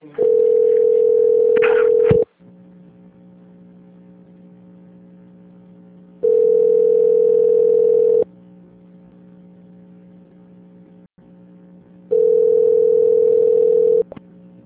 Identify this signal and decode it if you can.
Ringback tone (North America), 440 + 480 Hz.